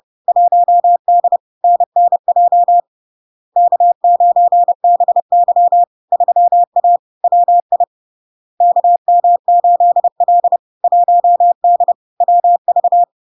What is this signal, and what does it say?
Morse code, heard as a keyed tone, about 30 wpm, 700 Hz. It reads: 1D NNJ K9BY 3A WI KM8L 1D WV
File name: LARK-FD-2022-Log-in-CW-30wpm-700-Hz